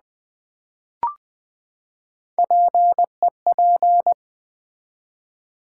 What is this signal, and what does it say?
Morse code, heard as a keyed tone, about 20 wpm, 700 Hz. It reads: PEP
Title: Build A Word - 3 Letters At A Time - 6-Letter Words - 20wpm